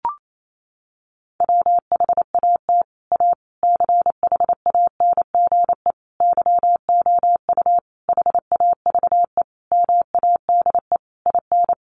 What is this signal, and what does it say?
Morse code, heard as a keyed tone, about 28 wpm, 700 Hz. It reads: WHAT A CHANGE YOU HAVE MADE IN
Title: Sentences from Top 600 Words 28wpm